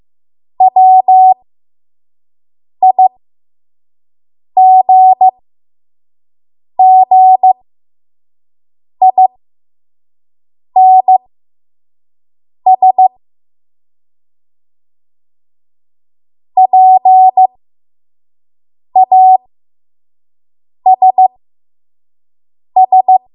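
Morse code, heard as a keyed tone, about 15 wpm, spaced as for 5 wpm, 750 Hz. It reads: WIGGINS PASS
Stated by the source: 5 WPM Morse code practice.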